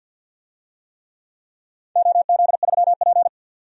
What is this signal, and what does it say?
Morse code, heard as a keyed tone, about 49 wpm, 700 Hz. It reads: OZ3P
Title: DXCC-100 Call Signs - 4 Characters - 50wpm